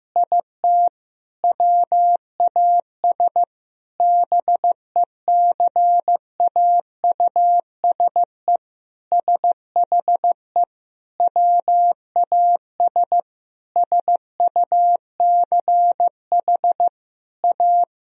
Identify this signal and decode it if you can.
Morse code, heard as a keyed tone, about 15 wpm, 700 Hz. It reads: IT WAS BECAUSE SHE WAS SUCH A
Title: Sentences from Top 1000 Words - Review Part 4 - 15wpm